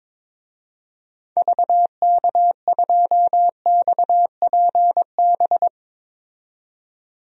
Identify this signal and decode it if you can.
Morse code, heard as a keyed tone, about 22 wpm, 700 Hz. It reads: VK2XPB
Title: DXCC-100 Call Signs - 6 Characters - 22wpm